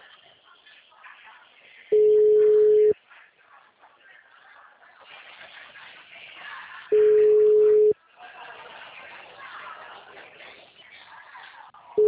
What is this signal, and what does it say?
Ringback tone (Europe, 425 Hz), with speech.